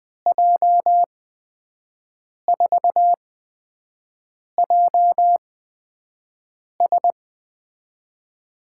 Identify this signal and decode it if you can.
Morse code, heard as a keyed tone, about 21 wpm, spaced as for 5 wpm, 700 Hz. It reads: J4JS
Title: USA Call Signs - 5 Characters - 8x Character Spacing - 20wpm